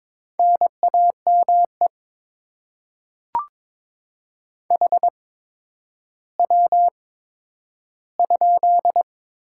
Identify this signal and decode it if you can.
Morse code, heard as a keyed tone, about 21 wpm, spaced as for 11 wpm, 700 Hz. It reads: NAME HW?
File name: Top 20 QSO Elements - 8x Character Spacing - 22wpm